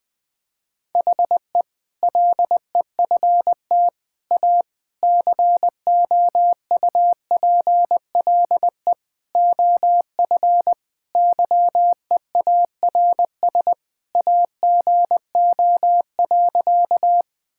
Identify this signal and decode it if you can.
Morse code, heard as a keyed tone, about 20 wpm, 700 Hz. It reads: HE LEFT A COUPLE OF YEARS AGO.
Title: Sentences with 4-Word Phrases - 20wpm